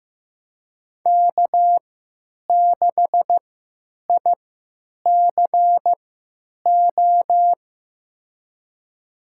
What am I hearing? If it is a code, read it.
Morse code, heard as a keyed tone, about 15 wpm, spaced as for 9 wpm, 700 Hz. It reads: K6ICO
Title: USA Call Signs - 5 Characters - 3x Character Spacing - 15wpm